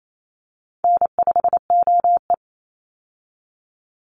Morse code, heard as a keyed tone, about 28 wpm, 700 Hz. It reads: N5OE
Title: Easy Call Signs - 28wpm